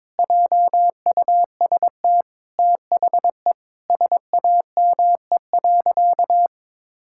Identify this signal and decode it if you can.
Morse code, heard as a keyed tone, about 22 wpm, 700 Hz. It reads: JUST THE SAME.
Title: Sentences from Top 1600 Words 22wpm